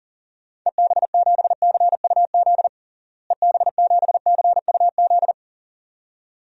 Morse code, heard as a keyed tone, about 40 wpm, 700 Hz. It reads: EB7CUZ EB7CUZ
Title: DXCC-100 Call Signs - 6 Characters - Repeated 2x - 40wpm